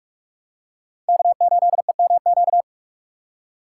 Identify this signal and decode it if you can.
Morse code, heard as a keyed tone, about 44 wpm, 700 Hz. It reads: K8EMQ